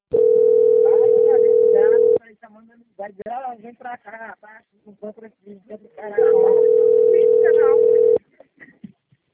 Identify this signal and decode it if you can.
Ringback tone (North America), 440 + 480 Hz, with speech.
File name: out- - - - - .2754